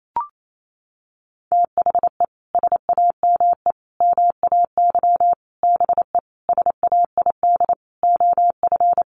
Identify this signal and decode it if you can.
Morse code, heard as a keyed tone, about 28 wpm, 700 Hz. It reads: THE SAME MAY BE SAID OF
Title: Sentences from Top 1100 Words 28wpm